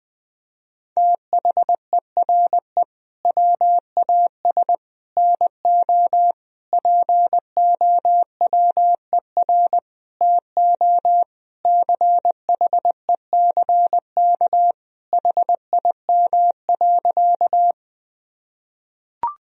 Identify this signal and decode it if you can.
Morse code, heard as a keyed tone, about 20 wpm, 700 Hz. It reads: THERE WAS NO POWER TO CHECK HIM.
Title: Sentences from Top 1000 Words - Review Part 3 - 20wpm